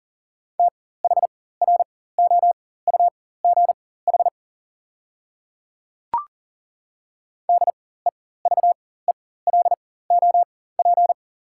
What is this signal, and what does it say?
Morse code, heard as a keyed tone, about 40 wpm, spaced as for 19 wpm, 690 Hz. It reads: THROUGH DEVELOP
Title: 4x Character Spacing - Top 500 Words - 40wpm